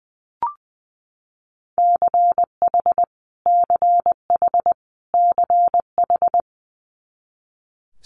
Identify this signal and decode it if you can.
Morse code, heard as a keyed tone, about 20 wpm, 700 Hz. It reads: CH CH CH
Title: Multi-letter Phonemes Repeated 3x 20wpm